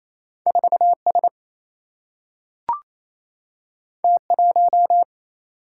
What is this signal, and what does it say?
Morse code, heard as a keyed tone, about 28 wpm, 700 Hz. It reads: 4S T1
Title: Single-Letter Single-Number 28wpm